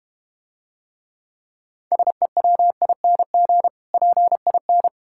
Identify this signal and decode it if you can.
Morse code, heard as a keyed tone, about 34 wpm, 700 Hz. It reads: SEWING PIN